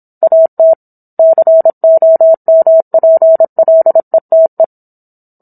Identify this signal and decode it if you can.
Morse code, heard as a keyed tone, about 27 wpm, 650 Hz. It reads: AT COMPLETE